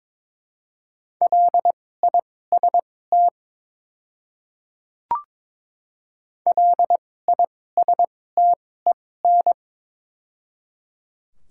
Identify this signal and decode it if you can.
Morse code, heard as a keyed tone, about 22 wpm, spaced as for 16 wpm, 700 Hz. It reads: LIST LISTEN
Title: 2x Character Spacing - Top 500 Words - 22wpm